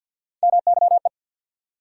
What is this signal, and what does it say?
Morse code, heard as a keyed tone, about 52 wpm, 700 Hz. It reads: MYE
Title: USA Call Sign Postfixes - 3 Letters 50wpm